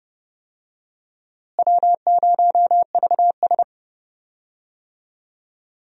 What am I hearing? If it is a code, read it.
Morse code, heard as a keyed tone, about 30 wpm, 700 Hz. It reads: W0VS
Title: USA Call Signs - 4 Characters - Repeated 2x - 30wpm